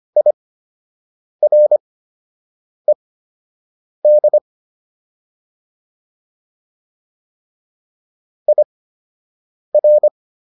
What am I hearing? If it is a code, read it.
Morse code, heard as a keyed tone, about 25 wpm, spaced as for 7 wpm, 600 Hz. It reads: IRED IR